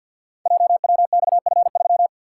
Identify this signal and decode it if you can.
Morse code, heard as a keyed tone, about 47 wpm, 700 Hz. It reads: JWXF3
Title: Code Groups - 5 Letter:Numbers - 50wpm